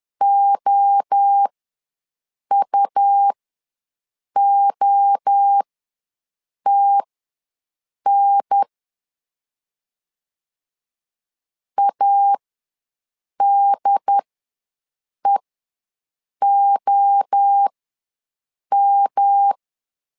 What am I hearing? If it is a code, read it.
Morse code, heard as a keyed tone, about 10 wpm, spaced as for 6 wpm, 800 Hz. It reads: OUOTN ADEOM
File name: LEZIONI / 3 MWRO / 3 MWRO ws 10wpm 2310